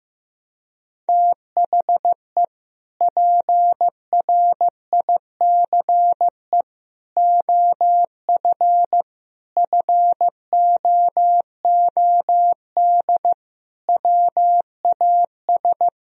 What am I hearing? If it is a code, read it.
Morse code, heard as a keyed tone, about 15 wpm, 700 Hz. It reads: THE PRICE OF FOOD WAS
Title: Sentences from Top 1000 Words - Review Part 4 - 15wpm